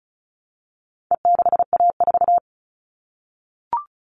Morse code, heard as a keyed tone, about 34 wpm, 700 Hz. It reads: E6A4
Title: All Cut Numbers - 4-Digit 35wpm